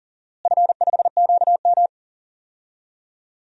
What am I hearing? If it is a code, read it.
Morse code, heard as a keyed tone, about 40 wpm, 700 Hz. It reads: F5QM
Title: DXCC-100 Call Signs - 4 Characters - Repeated 3x - 40wpm